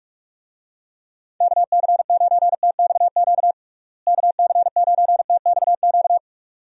Morse code, heard as a keyed tone, about 46 wpm, 700 Hz. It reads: KC9TXQ KC9TXQ
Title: USA Call Signs - 6 Characters - Repeated 2x - 45wpm